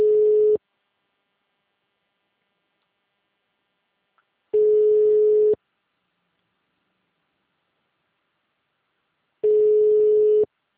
Ringback tone (Europe, 425 Hz).
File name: out- -2005- - - .856